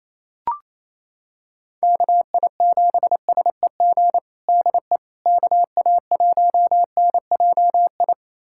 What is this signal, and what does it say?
Morse code, heard as a keyed tone, about 28 wpm, 700 Hz. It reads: KI7SEG DE KA1NJI
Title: USA Callsign DE Callsign - 6 Characters - 28wpm